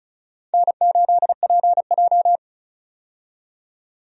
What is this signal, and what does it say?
Morse code, heard as a keyed tone, about 34 wpm, 700 Hz. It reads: N8PJ
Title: USA Call Signs - 4 Characters - 35wpm